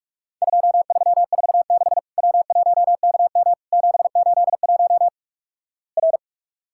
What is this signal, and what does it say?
Morse code, heard as a keyed tone, about 46 wpm, 700 Hz. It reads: 2346 W1KM 781 R
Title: Worked All Europe DX Contest - QTC Exchange - 45wpm